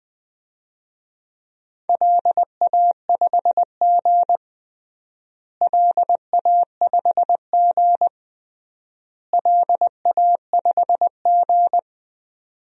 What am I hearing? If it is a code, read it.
Morse code, heard as a keyed tone, about 20 wpm, 700 Hz. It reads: LA5G LA5G LA5G
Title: DXCC-100 Call Signs - 4 Characters - Repeated 3x - 20wpm